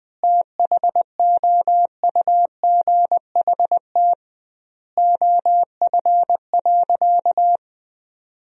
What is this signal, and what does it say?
Morse code, heard as a keyed tone, about 20 wpm, 700 Hz. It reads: THOUGHT OF.
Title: 2x Word-Spacing - Sentences from Top 300 Words - 20wpm